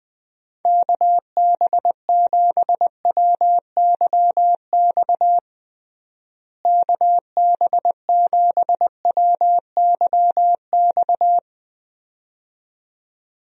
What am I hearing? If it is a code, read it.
Morse code, heard as a keyed tone, about 20 wpm, 700 Hz. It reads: KB7WYX KB7WYX
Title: USA Call Signs - 6 Characters - Repeated 2x - 20wpm